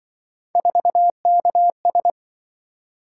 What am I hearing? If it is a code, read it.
Morse code, heard as a keyed tone, about 25 wpm, 700 Hz. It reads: 4KS
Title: Speed-Racing - Three Letter-Number Combinations - 20wpm